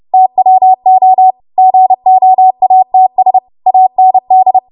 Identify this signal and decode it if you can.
Morse code, heard as a keyed tone, about 30 wpm, 750 Hz. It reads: TWO GOATS AND
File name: Morse / 30WPM